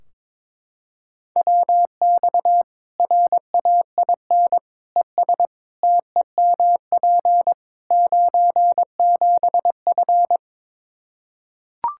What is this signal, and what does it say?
Morse code, heard as a keyed tone, about 22 wpm, 700 Hz. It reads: WX RAIN ES TEMP 97F
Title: Practice QSO - Exchange 2 - Info - 22wpm